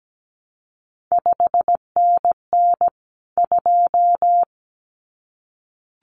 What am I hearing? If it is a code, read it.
Morse code, heard as a keyed tone, about 17 wpm, 700 Hz. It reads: 5NN 2
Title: CQ WW DX Contest Exchange Encore - 17wpm